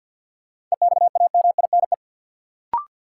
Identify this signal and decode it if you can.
Morse code, heard as a keyed tone, about 53 wpm, 700 Hz. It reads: EXAMINE
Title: Sentences from Top 1600 Words - New Word Drill - 50wpm